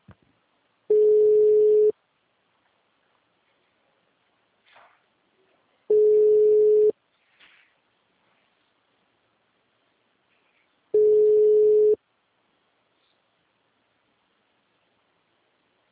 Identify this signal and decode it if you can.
Ringback tone (Europe, 425 Hz).
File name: out- -2038- - - .99651